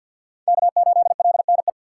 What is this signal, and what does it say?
Morse code, heard as a keyed tone, about 51 wpm, 700 Hz. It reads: K8LNE
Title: USA Call Signs - 5 Characters - 2x Character Spacing - 50wpm